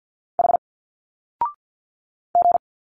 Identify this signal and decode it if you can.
Morse code, heard as a keyed tone, about 50 wpm, 700 Hz. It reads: H G
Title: Single Letters 50wpm